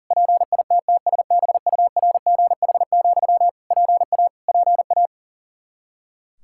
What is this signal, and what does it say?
Morse code, heard as a keyed tone, about 40 wpm, 700 Hz. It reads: PITTSBURGH, PA PA
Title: Practice QSO - Exchange 1 - Beginning - 40wpm